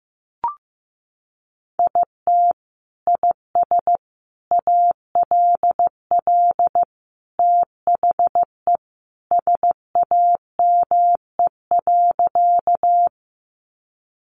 Morse code, heard as a keyed tone, about 15 wpm, 700 Hz. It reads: IT IS ALL THE SAME.
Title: Sentences from Top 200 Words 15wpm